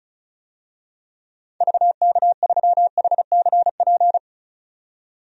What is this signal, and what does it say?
Morse code, heard as a keyed tone, about 36 wpm, 700 Hz. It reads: VK3HCP
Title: DXCC-100 Call Signs - 6 Characters - 35wpm